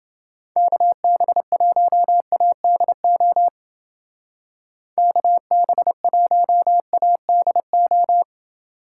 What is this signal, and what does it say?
Morse code, heard as a keyed tone, about 29 wpm, 700 Hz. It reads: KB1ADO KB1ADO